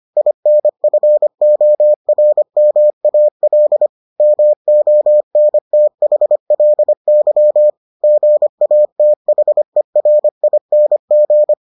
Morse code, heard as a keyed tone, about 25 wpm, 600 Hz. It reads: INFORMAL MONTHLY GATHERING